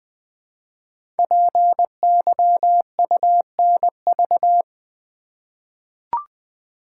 Morse code, heard as a keyed tone, about 20 wpm, 700 Hz. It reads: PYUNV
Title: Code Groups - 5 Letters - 20wpm